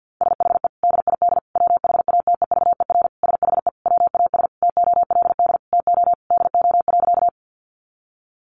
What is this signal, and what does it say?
Morse code, heard as a keyed tone, about 48 wpm, 700 Hz. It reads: SHE DID WHATEVER SHE WAS TOLD TO DO.
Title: Sentences from Top 1100 Words 50wpm